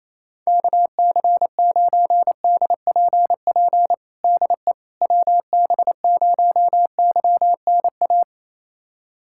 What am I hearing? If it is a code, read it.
Morse code, heard as a keyed tone, about 28 wpm, 700 Hz. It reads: KC9DPP DE WB0YNA
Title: USA Callsign DE Callsign - 6 Characters - 28wpm